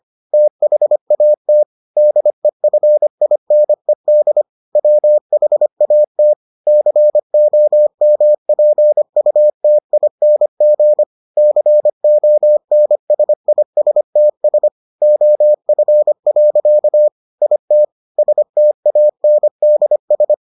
Morse code, heard as a keyed tone, about 25 wpm, 600 Hz. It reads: THAT DEFINED WHAT COMPUTING CONSISTS OF. IT STANDS